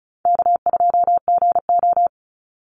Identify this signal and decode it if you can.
Morse code, heard as a keyed tone, about 34 wpm, 700 Hz. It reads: K2GO
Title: Easy Call Signs Repeated 2x - 35wpm